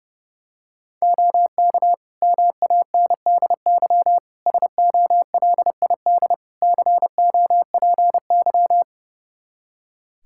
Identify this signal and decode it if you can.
Morse code, heard as a keyed tone, about 32 wpm, 700 Hz. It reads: OK MANDY SOLID COPY
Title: Practice QSO - Exchange 3 - More Info - 30wpm